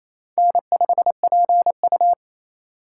Morse code, heard as a keyed tone, about 27 wpm, 700 Hz. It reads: N5PU
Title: K1USN SST QSO Practice - 28wpm